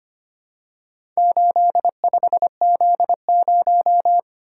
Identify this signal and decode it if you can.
Morse code, heard as a keyed tone, about 25 wpm, 700 Hz. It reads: 85Z0
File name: practice / 25wpm / Four Letter-Number Combinations - 25wpm